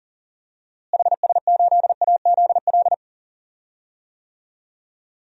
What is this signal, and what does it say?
Morse code, heard as a keyed tone, about 40 wpm, 700 Hz. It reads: HS8AZL
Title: DXCC-100 Call Signs - 6 Characters - Repeated 2x - 40wpm